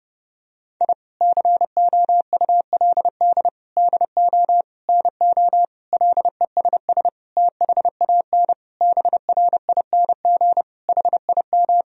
Morse code, heard as a keyed tone, about 29 wpm, 700 Hz. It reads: I COULD DO NO LESS THAN BRING HIM
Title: Sentences from Top 1000 Words - Review Part 3 - 30wpm